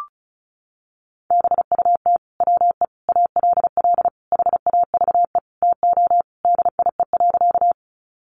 Morse code, heard as a keyed tone, about 34 wpm, 700 Hz. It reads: BUT WE ALL HAVE TO DIE.
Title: Sentences from Top 1300 Words 35wpm